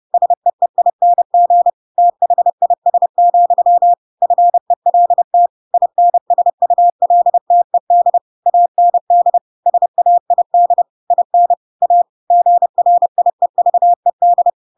Morse code, heard as a keyed tone, about 30 wpm, 700 Hz. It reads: SEEING THIS, FELT INSULTED AND SAID IN A GRIEVED
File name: Pinocchio / Pinocchio ch3 30 30 ik1hge